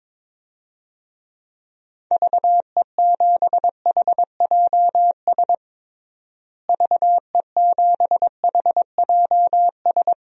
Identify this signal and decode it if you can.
Morse code, heard as a keyed tone, about 22 wpm, 700 Hz. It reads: VE7HJS VE7HJS
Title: DXCC-100 Call Signs - 6 Characters - Repeated 2x - 22wpm